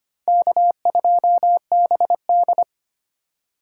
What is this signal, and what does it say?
Morse code, heard as a keyed tone, about 25 wpm, 700 Hz. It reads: K2BD
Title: USA Call Signs - 4 Characters - 2x Character Spacing - 25wpm